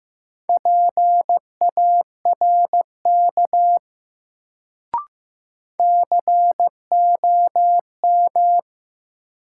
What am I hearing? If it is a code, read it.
Morse code, heard as a keyed tone, about 15 wpm, 700 Hz. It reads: PARK COM
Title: Sentences from Top 1300 Words - New Word Drill Alt - 15wpm